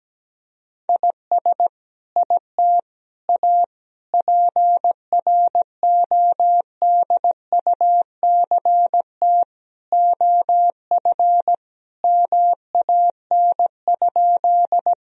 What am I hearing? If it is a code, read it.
Morse code, heard as a keyed tone, about 17 wpm, 700 Hz. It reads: IS IT A PRODUCT OF MAN?
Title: Sentences from Top 1000 Words - Review Part 2 - 17wpm